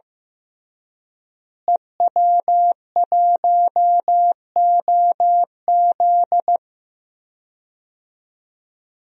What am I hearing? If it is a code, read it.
Morse code, heard as a keyed tone, about 16 wpm, 700 Hz. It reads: EW1OZ
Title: DXCC-100 Call Signs - 5 Characters - Repeated 3x - 15wpm